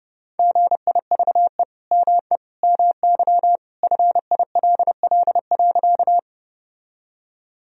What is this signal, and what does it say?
Morse code, heard as a keyed tone, about 30 wpm, 700 Hz. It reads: GIVE ME MY FILL.
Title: Sentences from Top 1000 Words - Review Part 3 - 30wpm